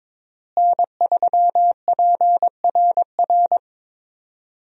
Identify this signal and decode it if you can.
Morse code, heard as a keyed tone, about 22 wpm, 700 Hz. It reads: N3PRR
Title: USA Call Signs - 5 Characters - Repeated 3x - 22wpm